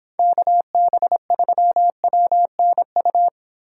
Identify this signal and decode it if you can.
Morse code, heard as a keyed tone, about 25 wpm, 700 Hz. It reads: KB3WNU